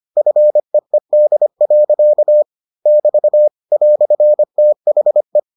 Morse code, heard as a keyed tone, about 25 wpm, 600 Hz. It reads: FEED. <BT> "THE